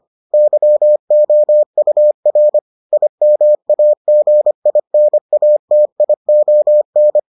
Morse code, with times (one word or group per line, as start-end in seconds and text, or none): 0.3-2.6 YOUR
2.9-7.2 IMAGINATION